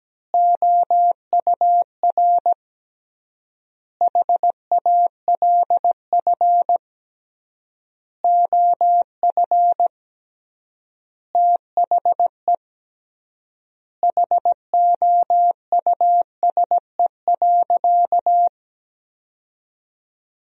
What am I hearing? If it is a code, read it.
Morse code, heard as a keyed tone, about 17 wpm, 700 Hz. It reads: OUR HALF OF THE HOUSE.
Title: 3x Word-Spacing - Sentences from Top 400 Words - 17wpm